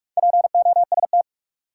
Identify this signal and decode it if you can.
Morse code, heard as a keyed tone, about 47 wpm, 700 Hz. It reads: POST
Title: Sentences from Top 1000 Words - New Word Drill - 45wpm